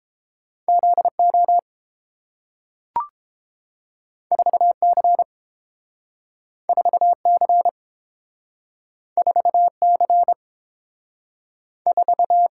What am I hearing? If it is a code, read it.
Morse code, heard as a keyed tone, about 30 wpm, 700 Hz. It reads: Z O 4 C 4 C 4 C 4